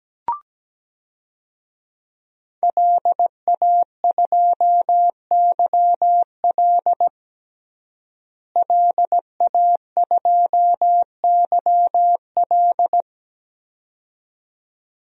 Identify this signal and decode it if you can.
Morse code, heard as a keyed tone, about 17 wpm, 700 Hz. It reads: LA2YL LA2YL
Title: DXCC-100 Call Signs - 5 Characters - Repeated 2x - 17wpm